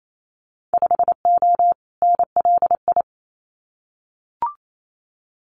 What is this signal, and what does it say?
Morse code, heard as a keyed tone, about 28 wpm, 700 Hz. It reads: 5O NLI
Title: Winter Field Day Exchanges 28wpm